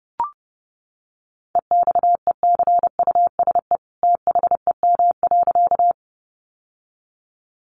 Morse code, heard as a keyed tone, about 30 wpm, 700 Hz. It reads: EXECUSE THEM.
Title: Sentences from Top 700 Words 30wpm